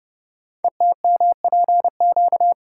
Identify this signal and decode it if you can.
Morse code, heard as a keyed tone, about 30 wpm, 700 Hz. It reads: ETMPQ
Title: Code Groups - 5 Letters - 30wpm